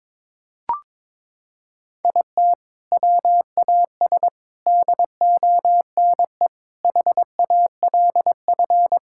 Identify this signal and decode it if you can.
Morse code, heard as a keyed tone, about 21 wpm, 700 Hz. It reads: IT WAS DONE HALF
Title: Sentences from Top 1000 Words - Review Part 5 - 22wpm